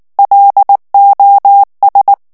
Morse code, heard as a keyed tone, about 19 wpm, 800 Hz. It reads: LOS
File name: morse mp3 / ESPAGNOL TEXTE EN CLAIR 2 VITESSE 18 MOTS MINUTES